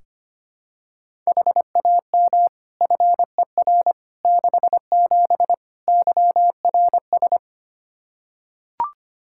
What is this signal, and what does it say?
Morse code, heard as a keyed tone, about 25 wpm, 700 Hz. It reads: HAM FER 67 YRS
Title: Practice QSO - Exchange 3 - More Info - 25wpm